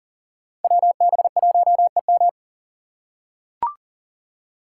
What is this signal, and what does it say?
Morse code, heard as a keyed tone, about 39 wpm, 700 Hz. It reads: WB1EM
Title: Code Groups - 5 Letter:Numbers - 40wpm